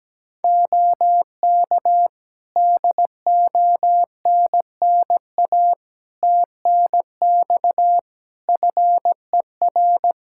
Morse code, heard as a keyed tone, about 17 wpm, 700 Hz. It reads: OK DONNA TNX FER
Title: Practice QSO - Exchange 4 - Ending - 17wpm